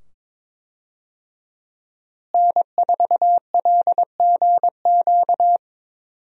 Morse code, heard as a keyed tone, about 22 wpm, 700 Hz. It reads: N4LGQ
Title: USA Call Signs - 5 Characters - Repeated 2x - 22wpm